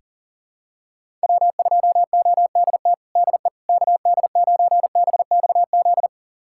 Morse code, heard as a keyed tone, about 40 wpm, 700 Hz. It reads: W2ODT DE KD9BXZ
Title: USA Callsign DE Callsign 40wpm